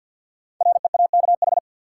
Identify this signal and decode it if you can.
Morse code, heard as a keyed tone, about 48 wpm, 700 Hz. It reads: REAKH
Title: Code Groups - 5 Letter:Numbers - 50wpm